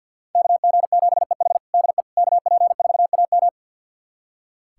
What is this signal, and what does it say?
Morse code, heard as a keyed tone, about 46 wpm, 700 Hz. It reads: KG7EH DE KP4AM
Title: USA Callsign DE Callsign - 5 Characters - 50wpm